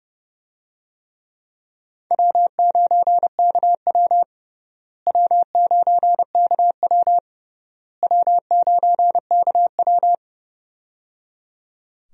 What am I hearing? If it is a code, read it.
Morse code, heard as a keyed tone, about 30 wpm, 700 Hz. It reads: W9KW W9KW W9KW
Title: USA Call Signs - 4 Characters - Repeated 3x - 30wpm